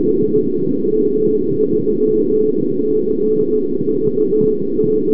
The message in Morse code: none